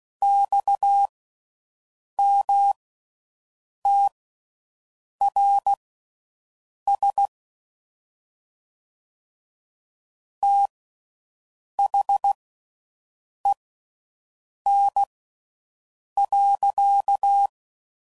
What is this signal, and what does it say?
Morse code, heard as a keyed tone, about 16 wpm, spaced as for 6 wpm, 800 Hz. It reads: XMTRS THEN.